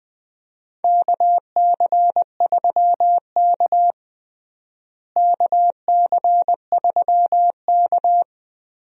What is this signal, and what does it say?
Morse code, heard as a keyed tone, about 20 wpm, 700 Hz. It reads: KC3K KC3K
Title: USA Call Signs - 4 Characters - Repeated 2x - 20wpm